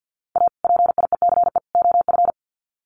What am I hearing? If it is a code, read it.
Morse code, heard as a keyed tone, about 50 wpm, 700 Hz. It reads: A PIECE OF
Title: Sets of 3 Words 50wpm